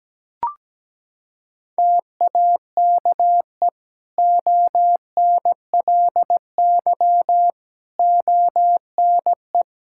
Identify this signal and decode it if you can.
Morse code, heard as a keyed tone, about 17 wpm, 700 Hz. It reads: TAKE ONLY ONE
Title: Sentences from Top 1000 Words - Review Part 5 - 17wpm